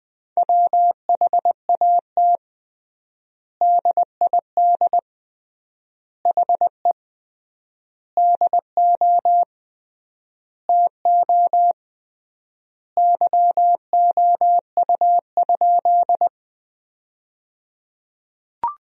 Morse code, heard as a keyed tone, about 20 wpm, 700 Hz. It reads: WHAT DID HE DO TO YOU?
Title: 3x Word-Spacing - Sentences from Top 100 Words - 20wpm